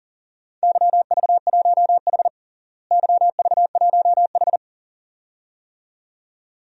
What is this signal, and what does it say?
Morse code, heard as a keyed tone, about 40 wpm, 700 Hz. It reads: YV1H YV1H
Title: DXCC-100 Call Signs - 4 Characters - Repeated 2x - 40wpm